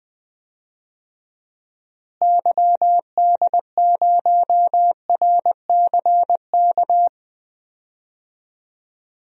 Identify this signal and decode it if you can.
Morse code, heard as a keyed tone, about 20 wpm, 700 Hz. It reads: YD0RCK
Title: DXCC-100 Call Signs - 6 Characters - Repeated 3x - 20wpm